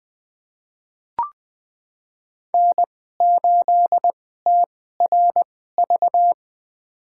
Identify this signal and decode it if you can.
Morse code, heard as a keyed tone, about 20 wpm, spaced as for 14 wpm, 700 Hz. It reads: N8TRV
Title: USA Call Signs - 5 Characters - 2x Character Spacing - 20wpm